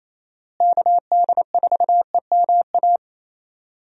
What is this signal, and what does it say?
Morse code, heard as a keyed tone, about 27 wpm, 700 Hz. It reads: KD4EMA
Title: USA Call Signs - 6 Characters - Repeated 2x - 28wpm